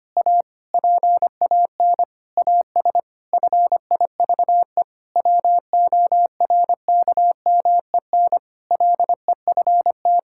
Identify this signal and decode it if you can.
Morse code, heard as a keyed tone, about 25 wpm, 700 Hz. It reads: A PAN AS FIVE WORKMEN LEFT
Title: Pangram Challenge v2 - 25wpm